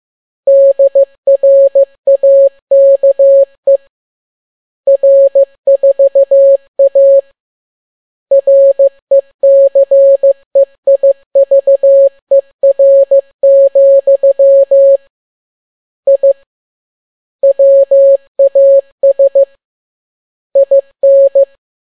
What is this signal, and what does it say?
Morse code, heard as a keyed tone, about 15 wpm, 550 Hz. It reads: DRAKE R4A RECEIVER, I WAS IN
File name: n6tt p1 15C13